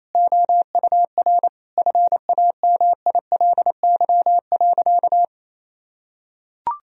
Morse code, heard as a keyed tone, about 28 wpm, 700 Hz. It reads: OUR FAMILY.
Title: Sentences from Top 1000 Words - Review Part 4 - 28wpm